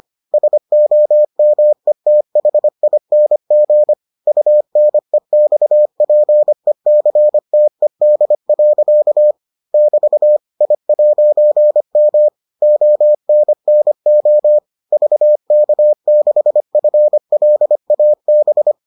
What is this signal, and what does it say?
Morse code, heard as a keyed tone, about 25 wpm, 600 Hz. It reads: SOMETHING UNEXPECTED. <BT> I'M ONNO VK6FLAB